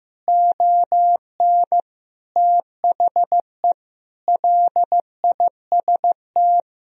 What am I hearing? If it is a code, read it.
Morse code, heard as a keyed tone, about 16 wpm, 700 Hz. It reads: ON THE LIST
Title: Sentences from Top 1000 Words - Review Part 2 - 15wpm